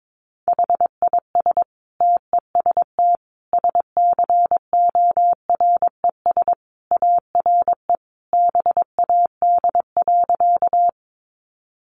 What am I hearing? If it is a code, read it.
Morse code, heard as a keyed tone, about 22 wpm, 700 Hz. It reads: HIS TEST SCORES ARE BAD.
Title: Additional Easy Phrases 22wpm